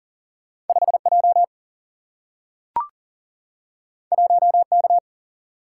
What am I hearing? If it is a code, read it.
Morse code, heard as a keyed tone, about 39 wpm, 700 Hz. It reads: 5J 1K
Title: Single-Letter Single-Number 40wpm